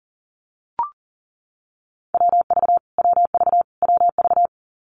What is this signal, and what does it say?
Morse code, heard as a keyed tone, about 40 wpm, 700 Hz. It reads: WV WV WV
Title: US States Repeated 3x 40wpm